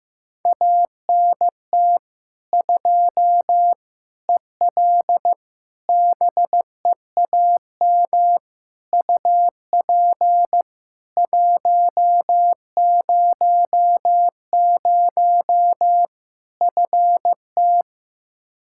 Morse code, heard as a keyed tone, about 15 wpm, 700 Hz. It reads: ANT 2 EL BEAM UP 100 FT
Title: Practice QSO - Exchange 2 - Info - 15wpm